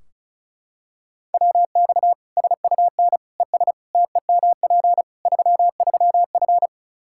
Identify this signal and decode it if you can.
Morse code, heard as a keyed tone, about 36 wpm, 700 Hz. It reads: WX SUN ES TEMP 33F
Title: K1USN SST QSO Practice 35wpm